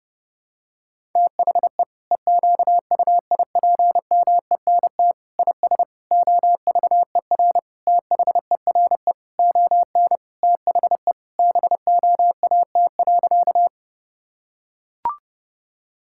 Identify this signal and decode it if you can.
Morse code, heard as a keyed tone, about 30 wpm, 700 Hz. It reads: THE EQUIPMENT IS OVER THERE ON THE BOAT.
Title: Sentences from Top 1000 Words - Review Part 5 - 30wpm